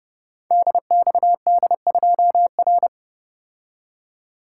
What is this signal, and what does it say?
Morse code, heard as a keyed tone, about 30 wpm, 700 Hz. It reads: DXD2R
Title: Code Groups - 5 Letter:Numbers - 30wpm